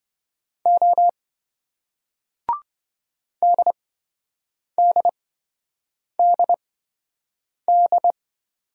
Morse code, heard as a keyed tone, about 27 wpm, spaced as for 7 wpm, 700 Hz. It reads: O DDDD